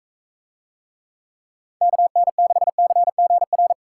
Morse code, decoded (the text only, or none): KN6CGR